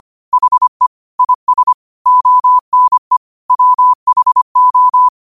Morse code, heard as a keyed tone, about 25 wpm, 1000 Hz. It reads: HE IS ONE WHO
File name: qotdfiles / 25wpm